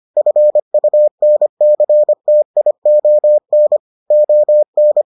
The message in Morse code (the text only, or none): FUNCTION ON